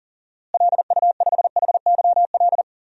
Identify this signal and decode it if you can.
Morse code, heard as a keyed tone, about 40 wpm, 700 Hz. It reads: LU5HYL
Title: DXCC-100 Call Signs - 6 Characters - Repeated 2x - 40wpm